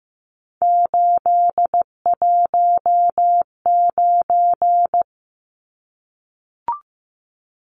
Morse code, heard as a keyed tone, about 15 wpm, 700 Hz. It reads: 819
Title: Numbers - 3-Digit 15wpm